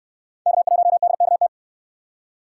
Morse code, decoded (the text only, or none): K1DKT